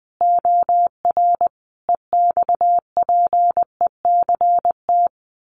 Morse code, heard as a keyed tone, about 19 wpm, 700 Hz. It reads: OR EXPECT
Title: Sentences from Top 1400 Words 20wpm